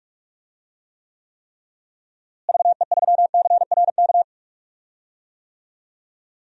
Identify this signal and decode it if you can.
Morse code, heard as a keyed tone, about 46 wpm, 700 Hz. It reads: VE3CRK
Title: DXCC-100 Call Signs - 6 Characters - Repeated 3x - 45wpm